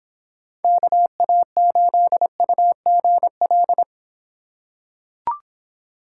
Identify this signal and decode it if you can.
Morse code, heard as a keyed tone, about 26 wpm, 700 Hz. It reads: KA8UGL